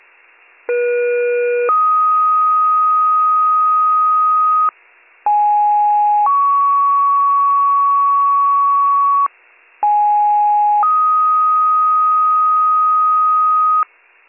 Two-tone page, heard as a one-second tone, then a three-second tone, about 480 then 1250 Hz, about 830 then 1120 Hz, about 830 then 1250 Hz. It is heard from 0.7 to 13.8 s.